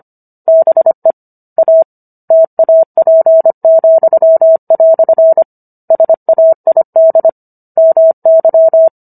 Morse code, heard as a keyed tone, about 25 wpm, 650 Hz. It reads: BE A TAP," SAID MY